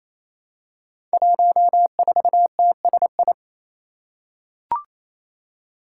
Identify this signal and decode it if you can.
Morse code, heard as a keyed tone, about 28 wpm, 700 Hz. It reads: 14TSI
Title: Five Challenging Letter-Number Combinations 28wpm